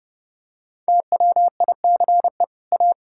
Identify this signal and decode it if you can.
Morse code, heard as a keyed tone, about 30 wpm, 700 Hz. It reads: TWICE A
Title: Sentences from Top 1600 Words 30wpm